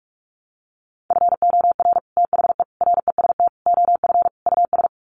Morse code, heard as a keyed tone, about 44 wpm, 700 Hz. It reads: FOR THE REST OF US
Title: Sets of 5 Words Encore 45wpm